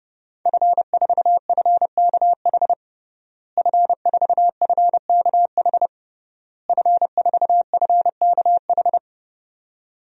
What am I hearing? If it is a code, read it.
Morse code, heard as a keyed tone, about 31 wpm, 700 Hz. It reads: F4FKH F4FKH F4FKH
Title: DXCC-100 Call Signs - 5 Characters - Repeated 3x - 30wpm